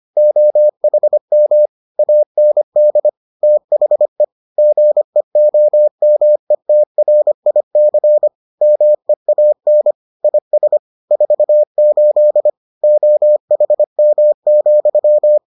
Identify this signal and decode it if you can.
Morse code, heard as a keyed tone, about 25 wpm, 600 Hz. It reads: OHM AND THE GEOMETRIC MEAN IS 48 OHM,